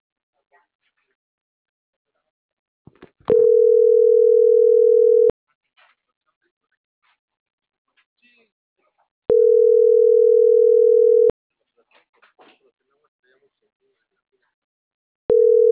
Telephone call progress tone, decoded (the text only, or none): ringback tone